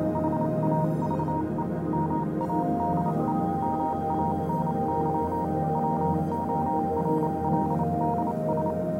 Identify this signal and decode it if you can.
Morse code, heard as a keyed tone, about 28 wpm, spaced as for 25 wpm, 940 Hz, with noise. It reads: HAVE MADE MANY WORLDS
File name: do you remember the song she sang (morse code version)